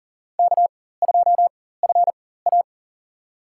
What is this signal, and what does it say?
Morse code, heard as a keyed tone, about 38 wpm, spaced as for 18 wpm, 700 Hz. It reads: K2FA
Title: USA Call Signs - 4 Characters - 4x Character Spacing - 40wpm